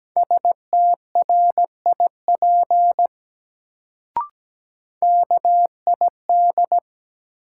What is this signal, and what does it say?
Morse code, heard as a keyed tone, about 17 wpm, 700 Hz. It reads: STRIP KID
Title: Sentences from Top 1600 Words - New Word Drill - 17wpm